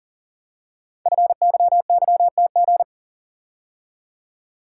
Morse code, heard as a keyed tone, about 40 wpm, 690 Hz. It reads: FYYTG
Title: Code Groups - 5 Letters - 40wpm